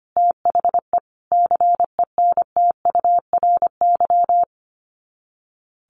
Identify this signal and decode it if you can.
Morse code, heard as a keyed tone, about 25 wpm, 700 Hz. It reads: THE CENTURY
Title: Sets of 5 Words 25wpm